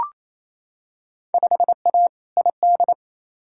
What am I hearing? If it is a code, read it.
Morse code, heard as a keyed tone, about 28 wpm, 700 Hz. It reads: 5A ID
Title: Field Day Exchanges Encore - 28wpm